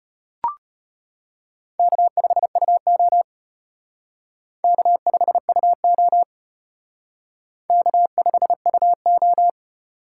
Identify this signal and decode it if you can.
Morse code, heard as a keyed tone, about 31 wpm, 700 Hz. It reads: K5UO K5UO K5UO
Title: Speed Racing - USA 1x2 Call Signs 25wpm